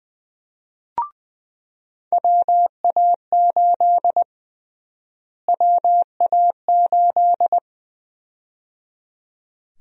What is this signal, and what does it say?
Morse code, heard as a keyed tone, about 20 wpm, 700 Hz. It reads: WA8 WA8
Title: USA Call Sign Prefixes - Repeated 2x - 20wpm